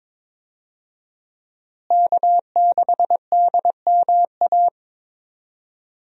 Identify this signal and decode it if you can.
Morse code, heard as a keyed tone, about 21 wpm, 700 Hz. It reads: K6DMA
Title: USA Call Signs - 5 Characters - Repeated 2x - 22wpm